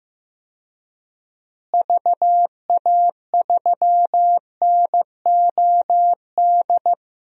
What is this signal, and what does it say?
Morse code, heard as a keyed tone, about 15 wpm, 700 Hz. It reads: VA3NOD
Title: DXCC-100 Call Signs - 6 Characters - Repeated 2x - 15wpm